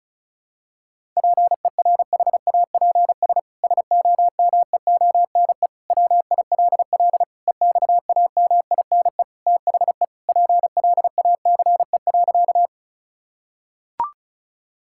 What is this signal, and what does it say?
Morse code, heard as a keyed tone, about 34 wpm, 700 Hz. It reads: PERHAPS SOMEONE WILL EXAMINE THE PLACE.
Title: Sentences from Top 1600 Words 35wpm